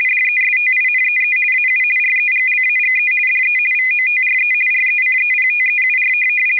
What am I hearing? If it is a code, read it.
RTTY, about 50 baud. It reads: DESTINEE AU REGLAGE DE VOTRE RECEPTEUR